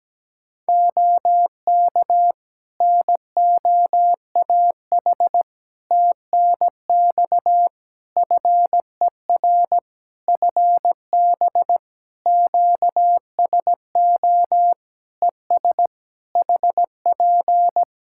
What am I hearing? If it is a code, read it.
Morse code, heard as a keyed tone, about 17 wpm, 700 Hz. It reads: OK NOAH TNX FER FB QSO ES HP
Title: Practice QSO - Exchange 4 - Ending - 17wpm